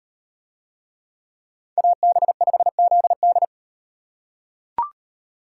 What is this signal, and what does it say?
Morse code, heard as a keyed tone, about 38 wpm, 700 Hz. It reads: AB5ZD